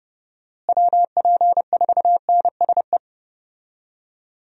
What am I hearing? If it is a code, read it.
Morse code, heard as a keyed tone, about 30 wpm, 700 Hz. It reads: WP4NSE
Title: USA Call Signs - 6 Characters - Repeated 2x - 30wpm